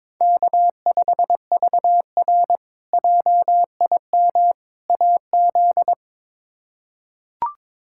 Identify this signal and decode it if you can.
Morse code, heard as a keyed tone, about 22 wpm, 700 Hz. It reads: K5VR JIM AZ
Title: K1USN SST QSO Practice - 22wpm